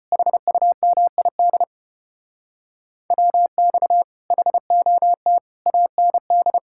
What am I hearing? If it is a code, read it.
Morse code, heard as a keyed tone, about 32 wpm, 700 Hz. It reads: HUMID WX HOT AND